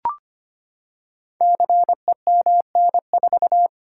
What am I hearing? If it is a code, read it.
Morse code, heard as a keyed tone, about 25 wpm, 700 Hz. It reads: CEMN4
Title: Code Groups - 5 Letter:Numbers - 25wpm